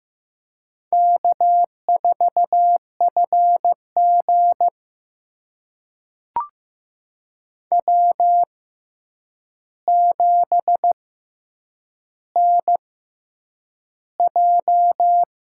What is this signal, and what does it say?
Morse code, heard as a keyed tone, about 15 wpm, spaced as for 8 wpm, 700 Hz. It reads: K4FG W7NJ
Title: USA Call Signs - 4 Characters - 6x Character Spacing - 15wpm